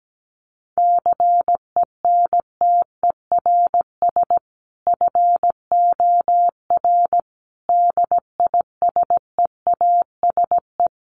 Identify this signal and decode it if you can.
Morse code, heard as a keyed tone, about 17 wpm, 700 Hz. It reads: CENTERS FOR DISEASE
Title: Sets of 5 Words Encore2 17wpm